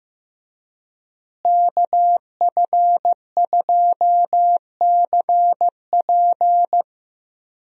Morse code, heard as a keyed tone, about 15 wpm, 700 Hz. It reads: KF2CP
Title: USA Call Signs - 5 Characters - Repeated 3x - 15wpm